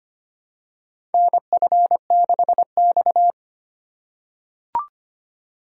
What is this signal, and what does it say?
Morse code, heard as a keyed tone, about 25 wpm, 700 Hz. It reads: NF6X
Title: USA Call Signs - 4x Character Spacing - 25wpm